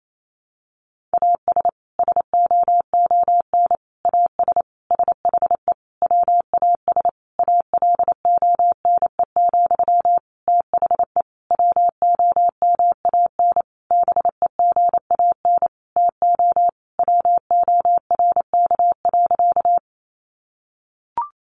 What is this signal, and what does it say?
Morse code, heard as a keyed tone, about 28 wpm, 700 Hz. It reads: AS SOON AS SHE WAS ALONE, THE WOMAN BEGAN TO WORK.
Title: Sentences from Top 1100 Words 28wpm